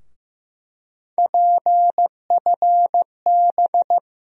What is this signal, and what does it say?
Morse code, heard as a keyed tone, about 15 wpm, 700 Hz. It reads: PFB
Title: USA Call Sign Postfixes - 3 Letters 15wpm